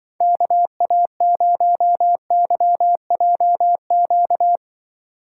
Morse code, heard as a keyed tone, about 24 wpm, 700 Hz. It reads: KA0YJQ
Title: Speed Racing - USA Call Signs 20wpm